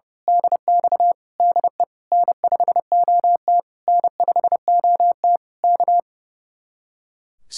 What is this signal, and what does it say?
Morse code, heard as a keyed tone, about 28 wpm, 700 Hz. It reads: DX DE N5OT N5OT K
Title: Speed-Racing Common QSO Phrases - 30wpm